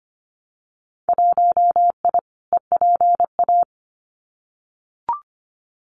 Morse code, heard as a keyed tone, about 25 wpm, 700 Hz. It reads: 1I EPA
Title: Winter Field Day Exchanges Encore 25wpm